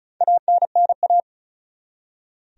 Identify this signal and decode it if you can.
Morse code, heard as a keyed tone, about 34 wpm, 700 Hz. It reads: ANNA
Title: USA Common Names - 35wpm